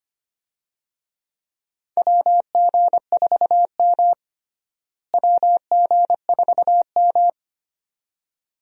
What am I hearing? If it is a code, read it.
Morse code, heard as a keyed tone, about 25 wpm, 700 Hz. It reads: WG4M WG4M
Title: USA Call Signs - 4 Characters - Repeated 2x - 25wpm